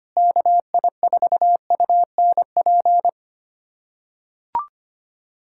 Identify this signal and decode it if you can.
Morse code, heard as a keyed tone, about 25 wpm, 700 Hz. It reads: KI4UNP
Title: USA Callsign DE Callsign - 6 Characters - 25wpm